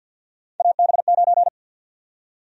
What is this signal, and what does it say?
Morse code, heard as a keyed tone, about 47 wpm, 700 Hz. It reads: AB9
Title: USA Call Sign Prefixes - Repeated 2x - 50wpm